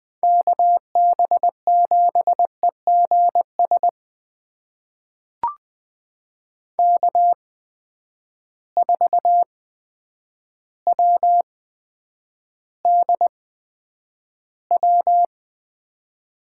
Morse code, heard as a keyed tone, about 20 wpm, spaced as for 9 wpm, 700 Hz. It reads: KB7EGS K4WDW
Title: USA Call Signs - 8x Character Spacing - 20wpm